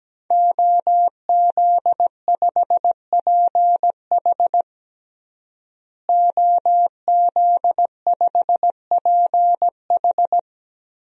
Morse code, heard as a keyed tone, about 17 wpm, 700 Hz. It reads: OZ5PH OZ5PH
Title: DXCC-100 Call Signs - 5 Characters - Repeated 3x - 17wpm